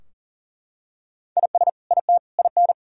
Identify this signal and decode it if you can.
Morse code, heard as a keyed tone, about 40 wpm, 700 Hz. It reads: IS IT IN
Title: Sentences from Top 1000 Words - Review Part 4 - 40wpm